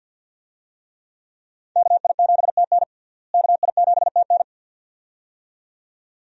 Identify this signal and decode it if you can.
Morse code, heard as a keyed tone, about 47 wpm, 700 Hz. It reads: KI7TN KI7TN
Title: USA Call Signs - 5 Characters - Repeated 2x - 50wpm